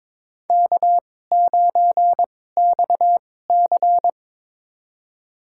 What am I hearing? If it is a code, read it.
Morse code, heard as a keyed tone, about 23 wpm, spaced as for 16 wpm, 700 Hz. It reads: K9XC
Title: USA Call Signs - 4 Characters - 2x Character Spacing - 22wpm